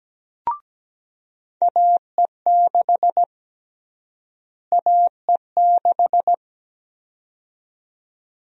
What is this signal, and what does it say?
Morse code, heard as a keyed tone, about 17 wpm, 700 Hz. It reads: AE6 AE6
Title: USA Call Sign Prefixes - Repeated 2x - 17wpm